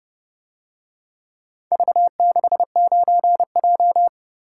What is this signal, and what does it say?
Morse code, heard as a keyed tone, about 30 wpm, 700 Hz. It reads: V69J